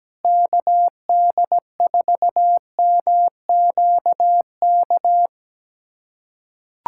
Morse code, heard as a keyed tone, about 17 wpm, 700 Hz. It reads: KD4MQK
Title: USA Callsign DE Callsign - 6 Characters - 17wpm